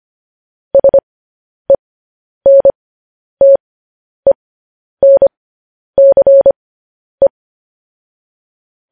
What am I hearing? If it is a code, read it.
Morse code, heard as a keyed tone, about 25 wpm, spaced as for 10 wpm, 550 Hz. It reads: SENTENCE